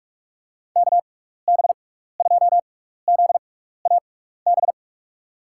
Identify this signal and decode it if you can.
Morse code, heard as a keyed tone, about 46 wpm, spaced as for 15 wpm, 700 Hz. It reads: KB2ZAB
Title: USA Call Signs - 6x Character Spacing - 45wpm